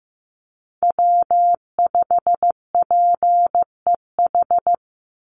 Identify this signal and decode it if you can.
Morse code, heard as a keyed tone, about 15 wpm, 700 Hz. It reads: W5PEH
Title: Call Signs Repeated 3x - 15wpm